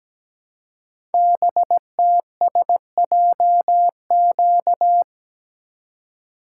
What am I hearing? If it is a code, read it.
Morse code, heard as a keyed tone, about 17 wpm, 700 Hz. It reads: BTSJQ
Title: Code Groups - 5 Letters - 17wpm